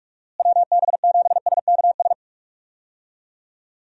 Morse code, heard as a keyed tone, about 47 wpm, 700 Hz. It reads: WB7SKS
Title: USA Call Signs - 6 Characters - Repeated 3x - 45wpm